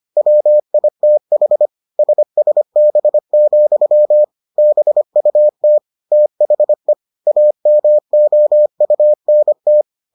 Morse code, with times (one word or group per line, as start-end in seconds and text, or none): 0.2-1.7 WITH
2.0-4.2 SSB,
4.6-5.8 BUT
6.1-6.9 THE
7.3-9.8 AMOUNT